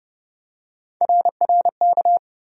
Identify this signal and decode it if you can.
Morse code, heard as a keyed tone, about 30 wpm, 700 Hz. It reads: RRK
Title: USA Call Sign Postfixes - 3 Letters 30wpm